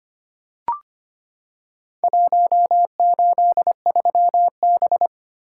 Morse code, heard as a keyed tone, about 25 wpm, 700 Hz. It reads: 183B
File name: practice / 25wpm / Four Letter-Number Combinations - 25wpm